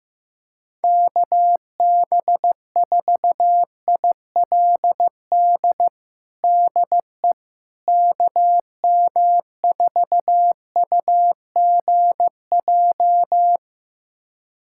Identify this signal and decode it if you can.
Morse code, heard as a keyed tone, about 15 wpm, 700 Hz. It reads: KB4ILD DE KM4UGJ
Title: USA Callsign DE Callsign - 6 Characters - 15wpm